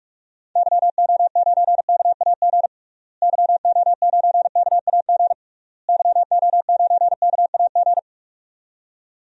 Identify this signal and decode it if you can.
Morse code, heard as a keyed tone, about 46 wpm, 700 Hz. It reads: YO9KAG YO9KAG YO9KAG
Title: DXCC-100 Call Signs - 6 Characters - Repeated 3x - 45wpm